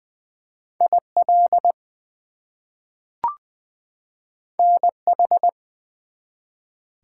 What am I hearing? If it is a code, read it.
Morse code, heard as a keyed tone, about 20 wpm, 700 Hz. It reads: IL NH
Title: US States and Canadian Provinces 20wpm